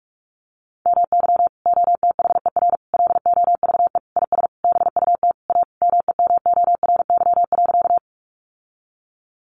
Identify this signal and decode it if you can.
Morse code, heard as a keyed tone, about 44 wpm, 700 Hz. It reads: MY OTHER LOVE IS BUT A MEMORY.
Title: Sentences from Top 1300 Words 45wpm